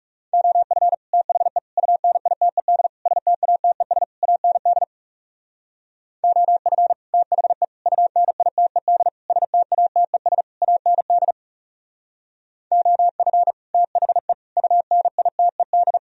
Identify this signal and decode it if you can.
Morse code, heard as a keyed tone, about 38 wpm, 700 Hz. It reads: OF THE UNITED STATES AND OF THE UNITED STATES AND OF THE UNITED
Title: Speed-Racing - Sets of 5 Words 30wpm